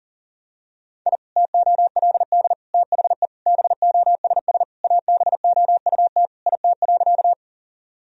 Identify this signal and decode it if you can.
Morse code, heard as a keyed tone, about 40 wpm, 700 Hz. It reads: I TOLD THE BOSS ABOUT IT.
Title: Sentences from Top 1000 Words - Review Part 4 - 40wpm